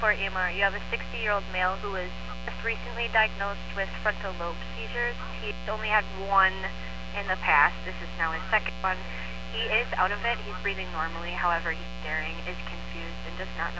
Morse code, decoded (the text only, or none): none